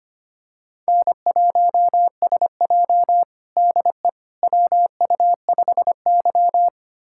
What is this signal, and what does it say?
Morse code, heard as a keyed tone, about 25 wpm, 700 Hz. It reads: N1SJ DE WU5Y
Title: USA Callsign DE Callsign - 4 Characters - 25wpm